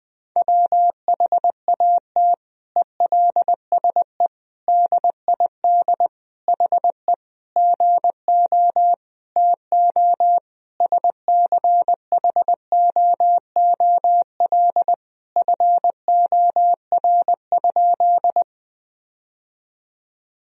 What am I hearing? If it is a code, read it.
Morse code, heard as a keyed tone, about 20 wpm, 700 Hz. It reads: WHAT ELSE DID HE GO TO SCHOOL FOR?
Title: Sentences from Top 1000 Words - Review Part 1 - 20wpm